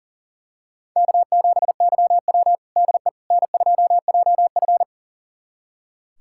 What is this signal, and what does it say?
Morse code, heard as a keyed tone, about 40 wpm, 700 Hz. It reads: K7YW DE N2JF
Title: USA Callsign DE Callsign - 4 Characters - 40wpm